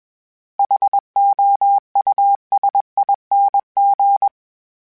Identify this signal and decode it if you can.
Morse code, heard as a keyed tone, about 21 wpm, 800 Hz. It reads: HOUSING